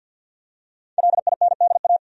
Morse code, heard as a keyed tone, about 46 wpm, 700 Hz. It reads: LINDA
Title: USA Common Names - 50wpm